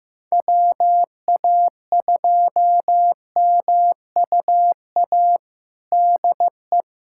Morse code, heard as a keyed tone, about 16 wpm, 700 Hz. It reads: WA2MUA DE
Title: USA Callsign DE Callsign 15wpm